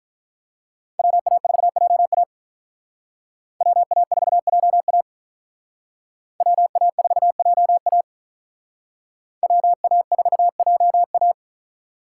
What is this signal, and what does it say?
Morse code, heard as a keyed tone, about 45 wpm, 700 Hz. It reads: WA4JA WA4JA WA4JA WA4JA